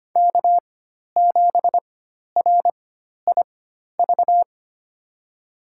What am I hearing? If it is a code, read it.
Morse code, heard as a keyed tone, about 25 wpm, spaced as for 12 wpm, 700 Hz. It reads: K7RIV
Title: USA Call Signs - 4x Character Spacing - 25wpm